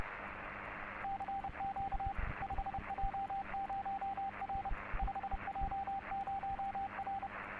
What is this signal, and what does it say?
Morse code, heard as a keyed tone, about 30 wpm, 790 Hz, with radio noise. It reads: CQ 5J0R 5J0R
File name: 5J0R - CQ 6m, CW/SSB